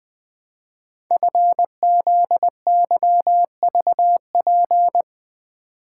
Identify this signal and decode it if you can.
Morse code, heard as a keyed tone, about 20 wpm, 700 Hz. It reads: FZYVP
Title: Code Groups - 5 Letter:Numbers - 20wpm